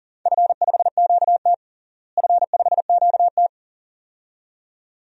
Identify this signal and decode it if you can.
Morse code, heard as a keyed tone, about 40 wpm, 700 Hz. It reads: F5QT F5QT
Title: DXCC-100 Call Signs - 4 Characters - Repeated 2x - 40wpm